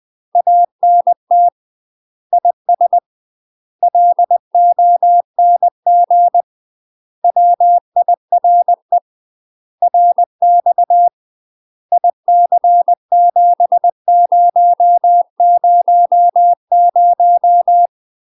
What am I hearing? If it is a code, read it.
Morse code, heard as a keyed tone, about 20 wpm, 700 Hz. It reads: ANT IS LONG WIRE RX IC7000
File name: UK MASTER QSOs TRAINING / 20WPM / UK MASTER QSOs 0045